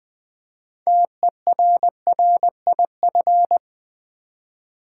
Morse code, heard as a keyed tone, about 20 wpm, 700 Hz. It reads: TERRIF
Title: Build A Word - 3 Letters At A Time - 8-Letter Words - 20wpm